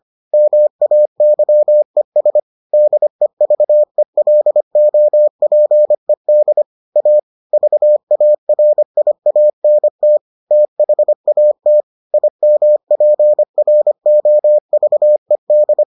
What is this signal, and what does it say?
Morse code, heard as a keyed tone, about 25 wpm, 600 Hz. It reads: MAYES DEVELOPED A VARIANT THAT IMPROVED